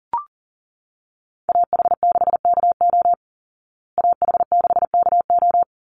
Morse code, heard as a keyed tone, about 40 wpm, 700 Hz. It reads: AH6KO AH6KO
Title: Call Signs Repeated 3x - 40wpm